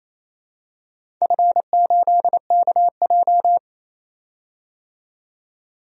Morse code, heard as a keyed tone, about 27 wpm, 700 Hz. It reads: F8KJ
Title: DXCC-100 Call Signs - 4 Characters - 28wpm